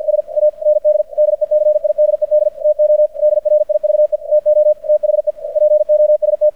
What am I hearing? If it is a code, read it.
Morse code, heard as a keyed tone, about 50 wpm, 600 Hz. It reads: S AND DECIDED TO DRIVE TO THE POST
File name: archived / CRSnet